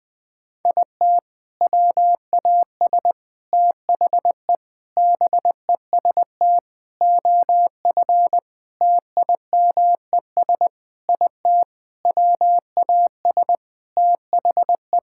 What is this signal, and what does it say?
Morse code, heard as a keyed tone, about 21 wpm, 700 Hz. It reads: IT WAS THE BEST OF TIMES IT WAS THE
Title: Sentences with 8-Word Phrases - Spoken Beginning - 20wpm